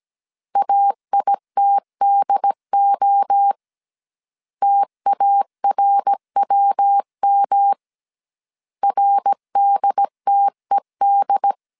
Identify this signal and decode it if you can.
Morse code, heard as a keyed tone, about 17 wpm, 800 Hz. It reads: AITDO TARWM RDTED